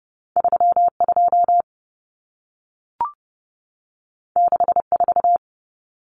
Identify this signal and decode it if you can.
Morse code, heard as a keyed tone, about 30 wpm, 700 Hz. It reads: 32 64
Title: Numbers - 2-Digit 30wpm